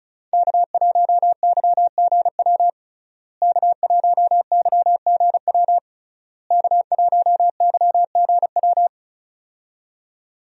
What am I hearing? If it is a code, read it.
Morse code, heard as a keyed tone, about 36 wpm, 700 Hz. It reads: K1YGW K1YGW K1YGW
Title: USA Call Signs - 5 Characters - Repeated 3x - 35wpm